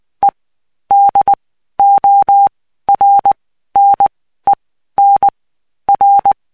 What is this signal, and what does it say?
Morse code, heard as a keyed tone, about 20 wpm, spaced as for 13 wpm, 800 Hz. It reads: E DORNEN R
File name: cw 75BPM